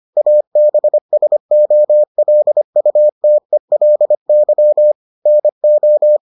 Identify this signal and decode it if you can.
Morse code, heard as a keyed tone, about 25 wpm, 600 Hz. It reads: ABSOLUTELY NO